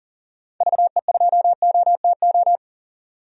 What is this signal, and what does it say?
Morse code, heard as a keyed tone, about 37 wpm, 700 Hz. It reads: VE2OTO
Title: DXCC-100 Call Signs - 6 Characters - 40wpm